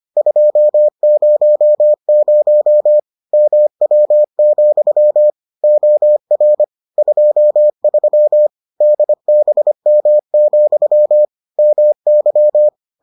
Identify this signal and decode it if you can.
Morse code, heard as a keyed tone, about 25 wpm, 600 Hz. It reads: 200 MW, OR 23 DBM, MY